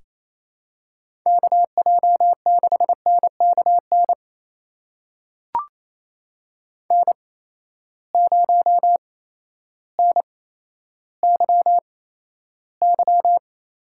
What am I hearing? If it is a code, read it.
Morse code, heard as a keyed tone, about 28 wpm, spaced as for 13 wpm, 700 Hz. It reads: KJ6NKN N0NYY
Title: USA Call Signs - 8x Character Spacing - 28wpm